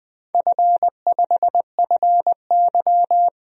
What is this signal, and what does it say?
Morse code, heard as a keyed tone, about 21 wpm, 700 Hz. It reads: F5FY
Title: DXCC-100 Call Signs - 4 Characters - Repeated 3x - 20wpm